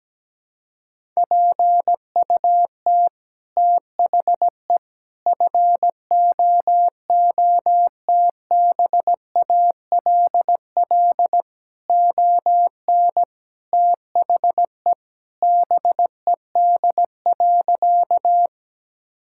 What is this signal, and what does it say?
Morse code, heard as a keyed tone, about 17 wpm, 700 Hz. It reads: PUT THE FOOTBALL ON THE BED.
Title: Sentences from Top 1600 Words 17wpm